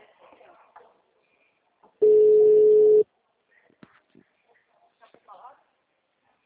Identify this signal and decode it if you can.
Ringback tone (Europe, 425 Hz), with speech.